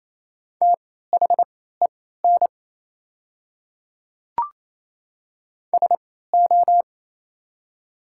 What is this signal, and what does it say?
Morse code, heard as a keyed tone, about 28 wpm, spaced as for 16 wpm, 700 Hz. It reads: THEN SO
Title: 3x Character Spacing - Top 100 Words - 28wpm